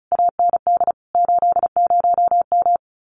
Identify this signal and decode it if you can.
Morse code, heard as a keyed tone, about 34 wpm, 700 Hz. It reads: AND 80M
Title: Common QSO Phrases Encore 35wpm